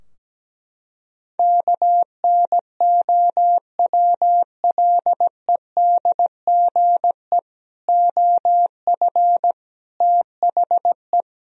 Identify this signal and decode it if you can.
Morse code, heard as a keyed tone, about 17 wpm, 700 Hz. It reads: KNOWLEDGE OF THE
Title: Sentences from Top 1000 Words - Review Part 3 - 17wpm